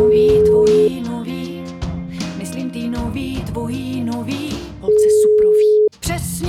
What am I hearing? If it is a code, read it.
Ringback tone (Europe, 425 Hz), with speech.